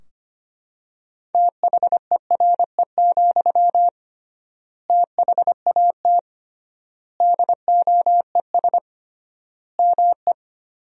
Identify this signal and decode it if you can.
Morse code, heard as a keyed tone, about 25 wpm, 700 Hz. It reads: THERE, THAT DOES ME
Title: 3x Word-Spacing - Sentences from Top 200 Words - 25wpm